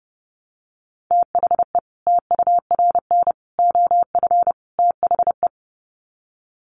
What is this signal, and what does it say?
Morse code, heard as a keyed tone, about 29 wpm, 700 Hz. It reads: THE TURN OF THE
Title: Sets of 4 Words 30wpm